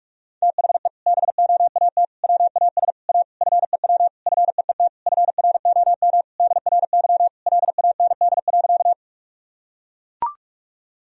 Morse code, heard as a keyed tone, about 41 wpm, 700 Hz. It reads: THE BOAT WAS A FEW FEET FROM DRY LAND.
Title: Sentences from Top 1000 Words - Review Part 3 - 45wpm